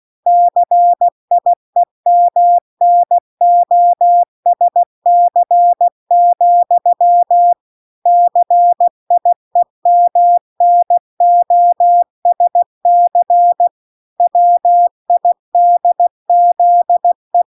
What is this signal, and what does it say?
Morse code, heard as a keyed tone, about 16 wpm, 700 Hz. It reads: CIEMNOSC, CIEMNOSC WIDZE